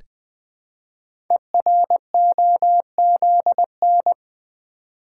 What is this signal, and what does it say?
Morse code, heard as a keyed tone, about 20 wpm, 700 Hz. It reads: EROZN
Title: Code Groups - 5 Letters - 20wpm